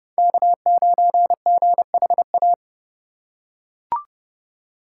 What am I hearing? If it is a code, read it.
Morse code, heard as a keyed tone, about 32 wpm, 700 Hz. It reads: K9GHA
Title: USA Call Signs - 5 Characters - 4x Character Spacing - 30wpm